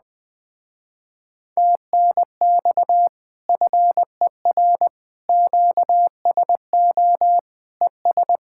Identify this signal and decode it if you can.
Morse code, heard as a keyed tone, about 20 wpm, 700 Hz. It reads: TNX FER QSO ES
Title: Speed-Racing Common QSO Phrases Encore - 20wpm